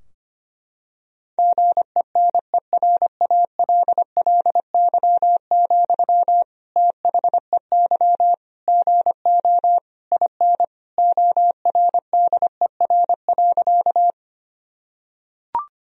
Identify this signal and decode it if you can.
Morse code, heard as a keyed tone, about 25 wpm, 700 Hz. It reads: GENERALLY, THEY GO IN ORDER.
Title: Sentences from Top 1000 Words - Review Part 4 - 25wpm